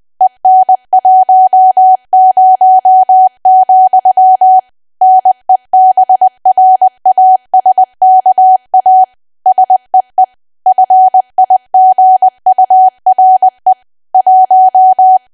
Morse code, heard as a keyed tone, about 20 wpm, 750 Hz. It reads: EN10, NEBRASKA SEE FIGURE 1
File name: cwa / W1AW-Level3 / 20WPM